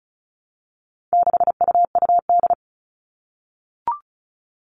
Morse code, heard as a keyed tone, about 34 wpm, 700 Hz. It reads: 6UUD
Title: All Cut Numbers - 4-Digit 35wpm